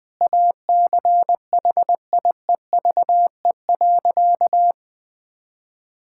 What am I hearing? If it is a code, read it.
Morse code, heard as a keyed tone, about 21 wpm, 700 Hz. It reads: ACHIEVE.
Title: Sentences from Top 1000 Words - Review Part 6 - 20wpm